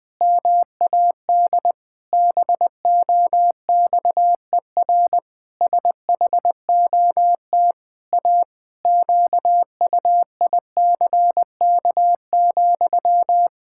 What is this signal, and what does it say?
Morse code, heard as a keyed tone, about 20 wpm, 700 Hz. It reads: MAD BOXER SHOT A QUICK,
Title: Pangram Challenge v2 - 20wpm